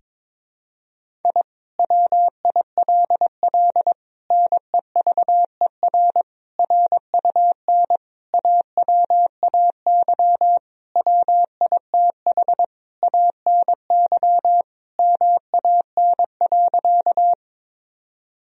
Morse code, heard as a keyed tone, about 22 wpm, 700 Hz. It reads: I WILL NEVER RUN AWAY WITH ANY MAN.
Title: Sentences from Top 1000 Words - Review Part 1 - 22wpm